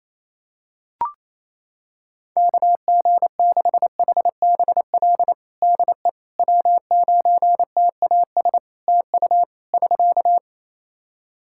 Morse code, heard as a keyed tone, about 28 wpm, 700 Hz. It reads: KG6HBL DE W9TAS TU <SK>
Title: Practice QSO - Exchange 4 - Ending - 28wpm